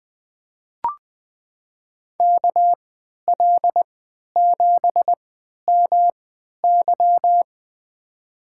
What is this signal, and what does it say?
Morse code, heard as a keyed tone, about 20 wpm, spaced as for 11 wpm, 700 Hz. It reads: KL7MY
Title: USA Call Signs - 5 Characters - 3x Character Spacing - 20wpm